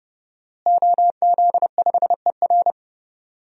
Morse code, heard as a keyed tone, about 30 wpm, 700 Hz. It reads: OZ5ER
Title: DXCC-100 Call Signs - 5 Characters - Repeated 3x - 30wpm